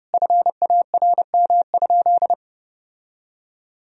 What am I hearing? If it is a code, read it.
Morse code, heard as a keyed tone, about 30 wpm, 700 Hz. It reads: FARM?
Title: Sentences from Top 1000 Words - Review Part 3 - 30wpm